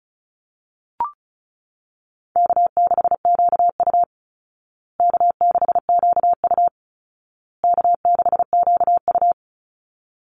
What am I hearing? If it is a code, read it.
Morse code, heard as a keyed tone, about 34 wpm, 700 Hz. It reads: K6QU K6QU K6QU
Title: Call Signs Repeated 2x - 35wpm